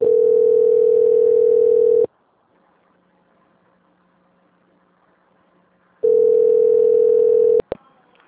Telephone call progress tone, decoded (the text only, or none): ringback tone